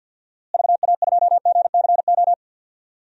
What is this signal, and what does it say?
Morse code, heard as a keyed tone, about 47 wpm, 700 Hz. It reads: VA2GCO
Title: DXCC-100 Call Signs - 6 Characters - Repeated 2x - 50wpm